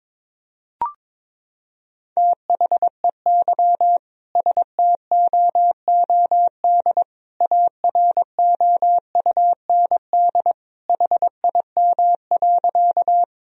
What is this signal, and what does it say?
Morse code, heard as a keyed tone, about 22 wpm, 700 Hz. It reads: THEY STOOD AROUND HIM.
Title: Sentences from Top 1000 Words - Review Part 3 - 22wpm